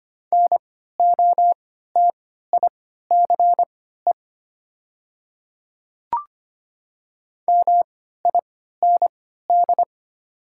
Morse code, heard as a keyed tone, about 25 wpm, spaced as for 14 wpm, 690 Hz. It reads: NOTICE MIND
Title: 3x Character Spacing - Top 500 Words - 25wpm